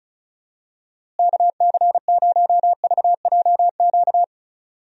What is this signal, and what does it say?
Morse code, heard as a keyed tone, about 36 wpm, 700 Hz. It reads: KC0VJQ
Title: USA Call Signs - 6 Characters - Repeated 3x - 35wpm